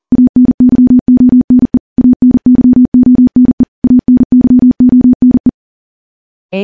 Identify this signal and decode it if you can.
Morse code, heard as a keyed tone, about 40 wpm, 250 Hz. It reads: ANYONE ANYONE ANYONE
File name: morse / morse-training-40WPM